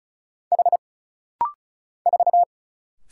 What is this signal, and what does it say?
Morse code, heard as a keyed tone, about 36 wpm, 700 Hz. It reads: H 4
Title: Single Letter-Number - Rapid-Fire 35wpm